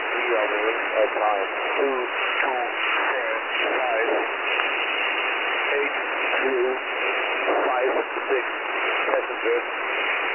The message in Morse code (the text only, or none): none